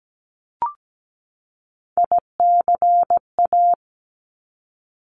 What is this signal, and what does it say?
Morse code, heard as a keyed tone, about 17 wpm, 700 Hz. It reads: ICA
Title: Top Three Letter Combinations in Words 17wpm